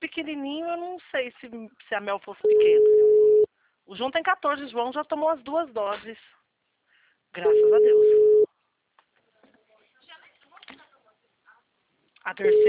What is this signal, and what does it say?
Ringback tone (Europe, 425 Hz), with speech.